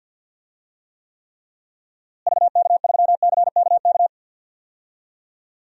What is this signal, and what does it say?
Morse code, heard as a keyed tone, about 50 wpm, 700 Hz. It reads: VK3CKK
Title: DXCC-100 Call Signs - 6 Characters - Repeated 3x - 50wpm